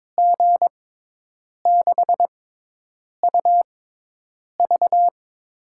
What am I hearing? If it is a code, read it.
Morse code, heard as a keyed tone, about 22 wpm, spaced as for 8 wpm, 700 Hz. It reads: G6UV
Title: USA Call Signs - 5 Characters - 6x Character Spacing - 22wpm